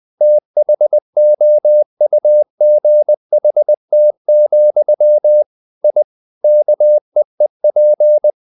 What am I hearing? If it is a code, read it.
Morse code, heard as a keyed tone, about 20 wpm, 600 Hz. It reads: THOUGHT, I KEEP
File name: the old man and the sea / the old man and the sea part 1 20wpm